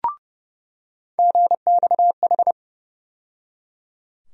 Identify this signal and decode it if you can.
Morse code, heard as a keyed tone, about 30 wpm, 700 Hz. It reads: GXH
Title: USA Call Sign Postfixes - 3 Letters 30wpm